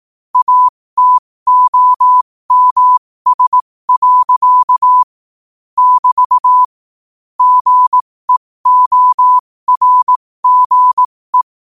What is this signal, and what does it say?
Morse code, heard as a keyed tone, about 18 wpm, spaced as for 15 wpm, 1000 Hz. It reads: ATOMS. <BT> GEORGE
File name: qotdfiles / 15wpm